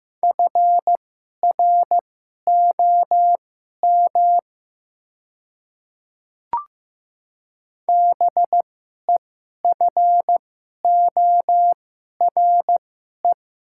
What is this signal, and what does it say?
Morse code, heard as a keyed tone, about 15 wpm, spaced as for 11 wpm, 700 Hz. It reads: FROM BEFORE
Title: 2x Character Spacing - Top 200 Words - 15wpm